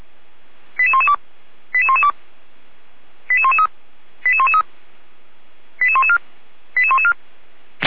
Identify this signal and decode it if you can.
Five-tone selective call (ZVEI), heard as quick runs of five tones, tones of about 70 ms. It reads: 88112 88112 88113 88113 88114 88114
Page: ZVEI